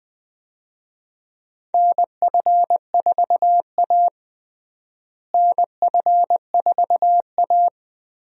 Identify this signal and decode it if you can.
Morse code, heard as a keyed tone, about 20 wpm, 700 Hz. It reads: NF4A NF4A
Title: USA Call Signs - 4 Characters - Repeated 2x - 20wpm